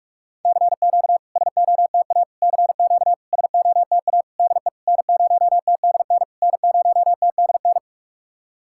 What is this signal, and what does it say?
Morse code, heard as a keyed tone, about 46 wpm, 690 Hz. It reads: CQ SOTA CQ SOTA DE N0TDN N0TDN
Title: SOTA QSO Practice 45wpm